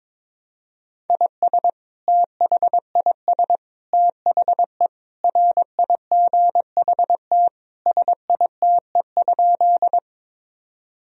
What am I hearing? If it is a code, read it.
Morse code, heard as a keyed tone, about 22 wpm, 700 Hz. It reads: IS THIS THE RIGHT SITE?
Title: Sentences from Top 1000 Words - Review Part 4 - 22wpm